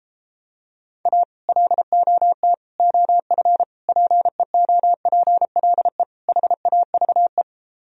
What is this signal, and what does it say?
Morse code, heard as a keyed tone, about 33 wpm, 700 Hz. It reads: A LOT OF PEOPLE HAVE